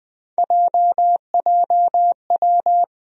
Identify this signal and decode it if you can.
Morse code, heard as a keyed tone, about 21 wpm, 700 Hz. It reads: JJW
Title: USA Call Sign Postfixes - 3 Letters 20wpm